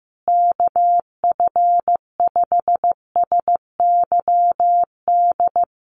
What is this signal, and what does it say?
Morse code, heard as a keyed tone, about 15 wpm, 700 Hz. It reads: KF5SYD
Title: USA Call Signs Repeated 2x 15wpm